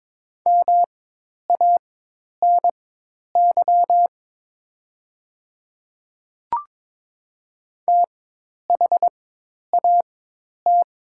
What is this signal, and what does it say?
Morse code, heard as a keyed tone, about 22 wpm, spaced as for 10 wpm, 700 Hz. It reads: MANY THAT
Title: 4x Character Spacing - Top 300 Words - 22wpm